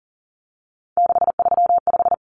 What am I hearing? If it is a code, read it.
Morse code, heard as a keyed tone, about 40 wpm, 700 Hz. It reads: 635
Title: Common Cut Numbers - 3-Digit 40wpm